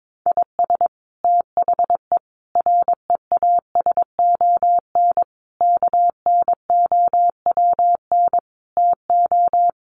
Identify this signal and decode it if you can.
Morse code, heard as a keyed tone, about 22 wpm, 700 Hz. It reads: IS THE REASON KNOWN TO
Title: Sentences from Top 700 Words 22wpm